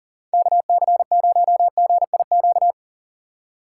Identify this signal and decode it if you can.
Morse code, heard as a keyed tone, about 40 wpm, 700 Hz. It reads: KC0GIQ
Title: USA Call Signs - 6 Characters - 3x Character Spacing - 40wpm